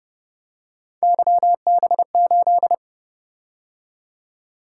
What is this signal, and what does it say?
Morse code, heard as a keyed tone, about 31 wpm, 700 Hz. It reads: YB8